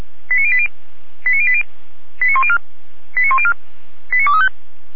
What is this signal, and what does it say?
Five-tone selective call (ZVEI), heard as quick runs of five tones, tones of about 70 ms. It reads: 89080 89080 89104 89104 89235
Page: /455/SDR/ZVEI/